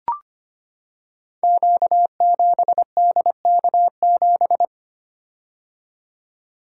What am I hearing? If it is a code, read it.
Morse code, heard as a keyed tone, about 25 wpm, 700 Hz. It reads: Q7DK7
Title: Code Groups - 5 Letter:Numbers - 25wpm